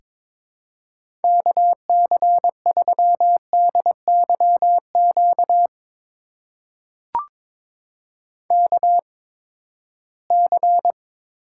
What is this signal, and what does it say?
Morse code, heard as a keyed tone, about 22 wpm, spaced as for 15 wpm, 700 Hz. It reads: KC3DYQ KC
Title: USA Call Signs - 6 Characters - 8x Character Spacing - 22wpm